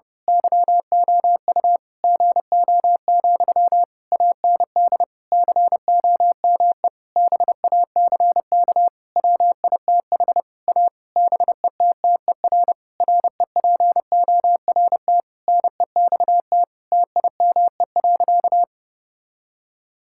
Morse code, heard as a keyed tone, about 32 wpm, 700 Hz. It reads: YOU GO, AND COME BACK WITH A BETTER REPORT NEXT TIME.
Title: Sentences from Top 1000 Words - Review Part 6 - 30wpm